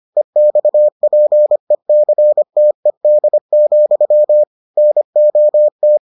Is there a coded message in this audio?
Yes: Morse code from 0.2 to 6.0 s, heard as a keyed tone, about 25 wpm, 600 Hz. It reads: EXPECTED, NOT